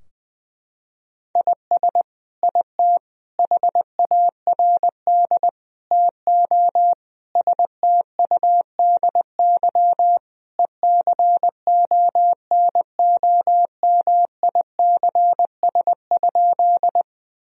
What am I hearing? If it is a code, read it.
Morse code, heard as a keyed tone, about 20 wpm, 700 Hz. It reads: IS IT HARD TO STUDY ECONOMICS?
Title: Sentences from Top 1000 Words - Review Part 3 - 20wpm